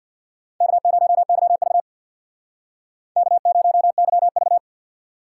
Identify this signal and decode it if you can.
Morse code, heard as a keyed tone, about 60 wpm, 700 Hz. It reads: K0YV K0YV
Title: Speed Racing - USA 1x2 Call Signs 50wpm